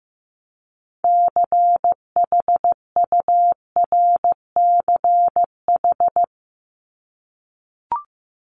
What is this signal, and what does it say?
Morse code, heard as a keyed tone, about 15 wpm, 700 Hz. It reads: CHURCH
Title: Top 1751-2000 Words 15wpm